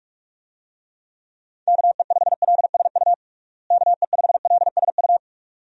Morse code, heard as a keyed tone, about 46 wpm, 700 Hz. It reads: KE5LSU KE5LSU
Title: USA Call Signs - 6 Characters - Repeated 2x - 45wpm